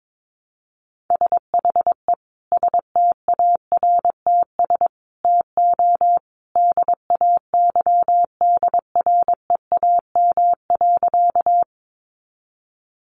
Morse code, heard as a keyed tone, about 22 wpm, 700 Hz. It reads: SHE STARTS TO DAYDREAM.
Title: Easy Phrases 22wpm